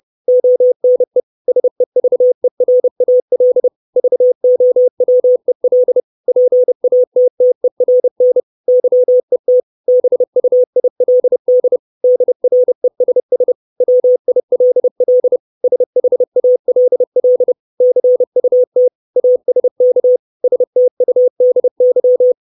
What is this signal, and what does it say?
Morse code, heard as a keyed tone, about 30 wpm, 500 Hz. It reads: ONE SEVERAL VOWEL PATTERN YET BUILD DRESS WILL SHALL CUT ASK STUDY
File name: 632 Words 30WPM